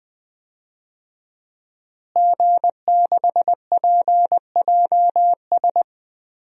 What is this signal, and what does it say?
Morse code, heard as a keyed tone, about 20 wpm, 700 Hz. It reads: G6PJS
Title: DXCC-100 Call Signs - 5 Characters - Repeated 2x - 20wpm